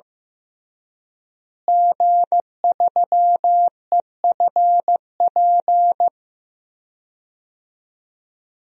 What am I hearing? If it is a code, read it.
Morse code, heard as a keyed tone, about 15 wpm, 700 Hz. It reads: G3EFP
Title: DXCC-100 Call Signs - 5 Characters - Repeated 3x - 15wpm